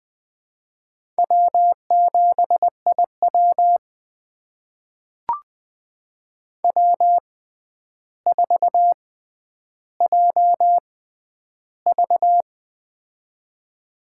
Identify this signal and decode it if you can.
Morse code, heard as a keyed tone, about 20 wpm, spaced as for 10 wpm, 700 Hz. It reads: W7IW W4JV
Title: USA Call Signs - 4 Characters - 6x Character Spacing - 20wpm